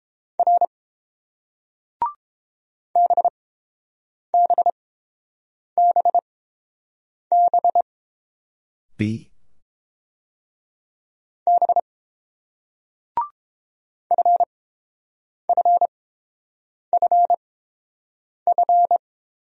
Morse code, heard as a keyed tone, about 30 wpm, spaced as for 7 wpm, 700 Hz. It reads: R BBBB B FFFF